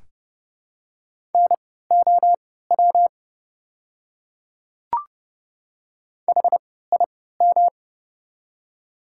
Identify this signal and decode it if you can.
Morse code, heard as a keyed tone, about 30 wpm, spaced as for 17 wpm, 690 Hz. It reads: NOW HIM
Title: 3x Character Spacing - Top 100 Words - 30wpm